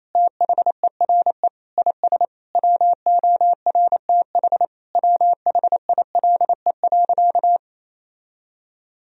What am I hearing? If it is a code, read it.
Morse code, heard as a keyed tone, about 28 wpm, 700 Hz. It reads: THERE IS WORTH WHILE.
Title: Sentences from Top 1000 Words - Review Part 1 - 28wpm